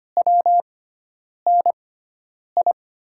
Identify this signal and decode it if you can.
Morse code, heard as a keyed tone, about 25 wpm, spaced as for 9 wpm, 700 Hz. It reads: WNI
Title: USA Call Signs - 5 Characters - 6x Character Spacing - 25wpm